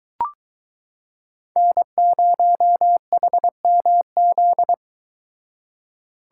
Morse code, heard as a keyed tone, about 22 wpm, 700 Hz. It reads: N0HMZ